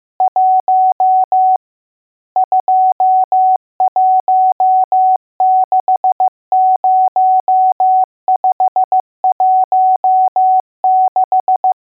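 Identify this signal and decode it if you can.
Morse code, heard as a keyed tone, about 15 wpm, 750 Hz. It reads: 1 2160516
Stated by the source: Morse code lesson: test on all digits.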